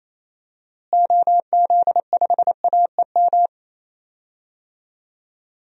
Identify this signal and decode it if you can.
Morse code, heard as a keyed tone, about 28 wpm, 700 Hz. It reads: OZ5AEM
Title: DXCC-100 Call Signs - 6 Characters - Repeated 3x - 28wpm